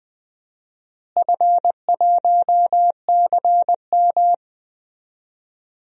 Morse code, heard as a keyed tone, about 20 wpm, 700 Hz. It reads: F1CM
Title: DXCC-100 Call Signs - 4 Characters - Repeated 2x - 20wpm